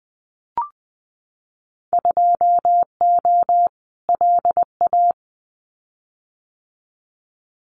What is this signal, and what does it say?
Morse code, heard as a keyed tone, about 20 wpm, 700 Hz. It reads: 2O LA
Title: Winter Field Day Exchanges 20wpm